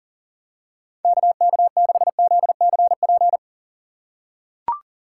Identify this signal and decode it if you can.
Morse code, heard as a keyed tone, about 40 wpm, 700 Hz. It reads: KK6ZCP
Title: USA Call Signs - 3x Character Spacing - 40wpm